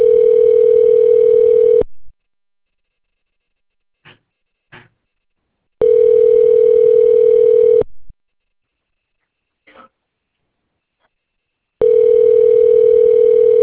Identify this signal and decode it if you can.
Ringback tone (North America), 440 + 480 Hz.